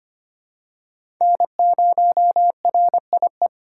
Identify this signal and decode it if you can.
Morse code, heard as a keyed tone, about 25 wpm, 700 Hz. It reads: N0RIE
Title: USA Call Signs - 5 Characters - 2x Character Spacing - 25wpm